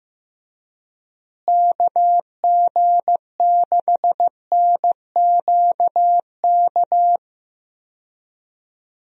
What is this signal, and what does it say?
Morse code, heard as a keyed tone, about 16 wpm, 700 Hz. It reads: KG6NQK
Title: USA Call Signs - 6 Characters - Repeated 2x - 15wpm